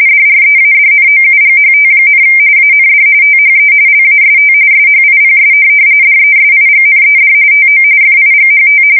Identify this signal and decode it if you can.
RTTY, about 45.45 baud. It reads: DREADNOUGHT, AND THE LAST SHIPS DESIGNED TO FIGHT
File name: fsk170 / USS-New-Jersey